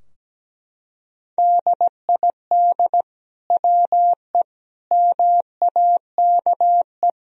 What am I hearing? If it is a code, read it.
Morse code, heard as a keyed tone, about 17 wpm, 700 Hz. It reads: DID WE MAKE
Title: Sentences from Top 1000 Words - Review Part 3 - 17wpm